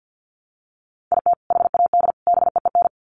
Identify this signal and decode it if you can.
Morse code, heard as a keyed tone, about 50 wpm, 700 Hz. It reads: IT HAD BEEN
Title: Sets of 3 Words 50wpm